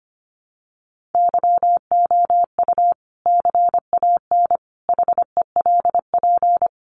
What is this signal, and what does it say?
Morse code, heard as a keyed tone, about 25 wpm, 700 Hz. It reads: YOU CAN HELP
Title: Sentences from Top 200 Words 25wpm